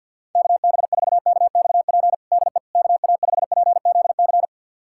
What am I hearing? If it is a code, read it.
Morse code, heard as a keyed tone, about 52 wpm, 700 Hz. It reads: KB4KXP DE KA5PZC
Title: USA Callsign DE Callsign - 6 Characters - 50wpm